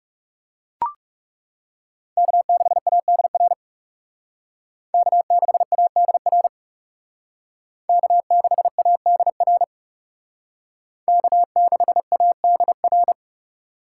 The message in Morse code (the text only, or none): K6ADR K6ADR K6ADR K6ADR